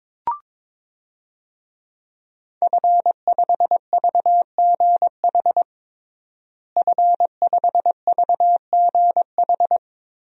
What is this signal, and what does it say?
Morse code, heard as a keyed tone, about 22 wpm, 700 Hz. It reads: F5VGH F5VGH
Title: DXCC-100 Call Signs - 5 Characters - Repeated 3x - 22wpm